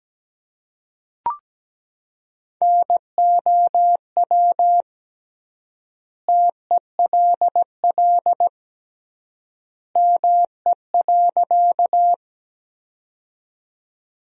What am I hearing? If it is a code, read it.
Morse code, heard as a keyed tone, about 17 wpm, 700 Hz. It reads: NOW TELL ME.
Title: 3x Word-Spacing - Sentences from Top 200 Words - 17wpm